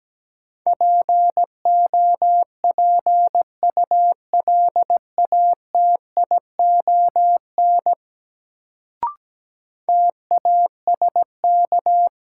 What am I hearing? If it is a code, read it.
Morse code, heard as a keyed tone, about 17 wpm, 700 Hz. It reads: POPULATION TASK
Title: Sentences from Top 1000 Words - Review Part 7 - New Word Drill - 17wpm